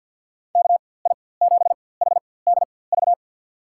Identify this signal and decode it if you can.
Morse code, heard as a keyed tone, about 53 wpm, spaced as for 24 wpm, 700 Hz. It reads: KI7HDV
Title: USA Call Signs - 6 Characters - 4x Character Spacing - 50wpm